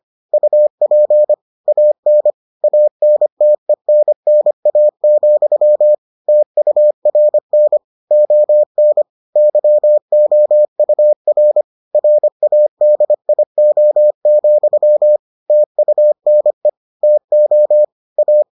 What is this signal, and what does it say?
Morse code, heard as a keyed tone, about 25 wpm, 600 Hz. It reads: UP AN ANTENNA, TURN ON YOUR RADIO, TUNE TO A